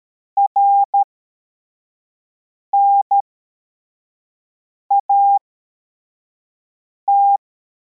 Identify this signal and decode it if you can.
Morse code, heard as a keyed tone, about 13 wpm, spaced as for 4 wpm, 800 Hz. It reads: RNAT